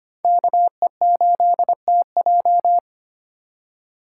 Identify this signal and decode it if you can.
Morse code, heard as a keyed tone, about 25 wpm, 700 Hz. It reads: KE8TJ
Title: USA Call Signs - 5 Characters - 2x Character Spacing - 25wpm